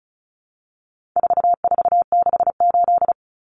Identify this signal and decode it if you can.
Morse code, heard as a keyed tone, about 34 wpm, 700 Hz. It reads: 4468
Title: Numbers - 4-Digit 35wpm